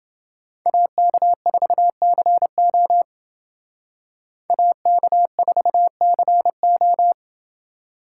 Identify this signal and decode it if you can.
Morse code, heard as a keyed tone, about 30 wpm, 700 Hz. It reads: AK4CO AK4CO